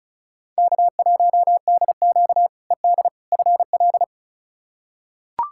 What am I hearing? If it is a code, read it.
Morse code, heard as a keyed tone, about 34 wpm, 700 Hz. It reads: K1DQ ED FL
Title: K1USN SST QSO Practice - 35wpm